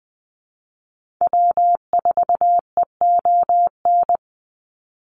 Morse code, heard as a keyed tone, about 20 wpm, 700 Hz. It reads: W4EON
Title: NA SOTA Activator Callsigns 20wpm